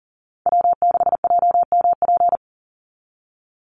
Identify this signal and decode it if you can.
Morse code, heard as a keyed tone, about 39 wpm, 700 Hz. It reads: W6JMP
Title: NA SOTA Chaser Callsigns 40wpm